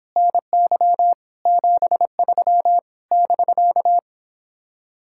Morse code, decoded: NY 73 <BK>

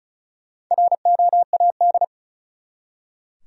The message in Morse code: ROAD

